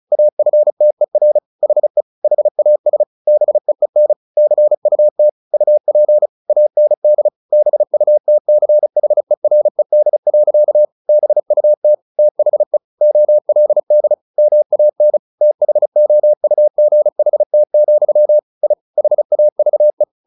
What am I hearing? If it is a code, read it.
Morse code, heard as a keyed tone, about 34 wpm, 600 Hz. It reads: AFTER HE HAS BEEN CUT UP AND BUTCHERED. BUT THE OLD MAN THOUGHT, I HAVE
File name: the old man and the sea / the old man and the sea 1 35wpm